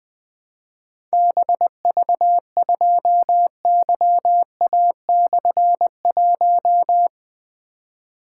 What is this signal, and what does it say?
Morse code, heard as a keyed tone, about 21 wpm, 700 Hz. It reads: BV2YA/1
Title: Difficult Call Signs 20wpm